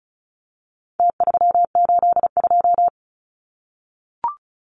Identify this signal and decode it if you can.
Morse code, heard as a keyed tone, about 34 wpm, 700 Hz. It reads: T382
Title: Numbers - Contest Style - 4-Digit 35wpm